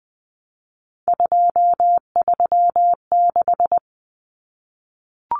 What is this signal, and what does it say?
Morse code, heard as a keyed tone, about 20 wpm, 700 Hz. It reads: 236
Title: Numbers - 3-Digit 20wpm